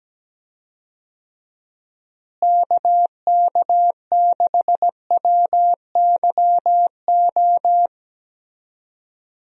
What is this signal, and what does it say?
Morse code, heard as a keyed tone, about 17 wpm, 700 Hz. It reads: KK6WYO
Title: USA Call Signs - 6 Characters - Repeated 2x - 17wpm